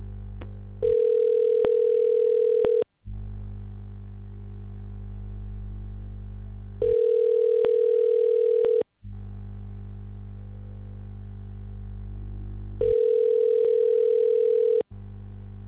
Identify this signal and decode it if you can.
Ringback tone (North America), 440 + 480 Hz.